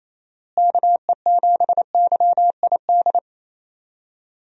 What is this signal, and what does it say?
Morse code, heard as a keyed tone, about 28 wpm, 700 Hz. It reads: KE7YID
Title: USA Call Signs - 6 Characters - Repeated 3x - 28wpm